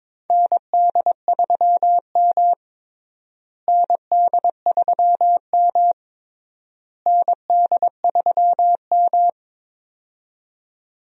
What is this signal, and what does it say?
Morse code, heard as a keyed tone, about 22 wpm, 700 Hz. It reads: ND3M ND3M ND3M
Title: USA Call Signs - 4 Characters - Repeated 3x - 22wpm